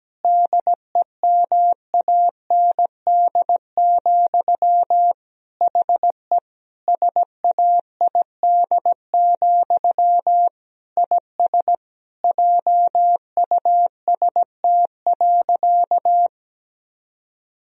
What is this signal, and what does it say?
Morse code, heard as a keyed tone, about 17 wpm, 700 Hz. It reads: DEMAND, HE SAID, IS JUST.
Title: Sentences from Top 1000 Words - Review Part 4 - 17wpm